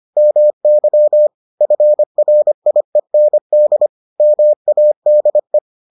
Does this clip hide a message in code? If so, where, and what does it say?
Yes: Morse code from 0.2 to 5.6 s, heard as a keyed tone, about 25 wpm, 600 Hz. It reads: MY FRIEND MADE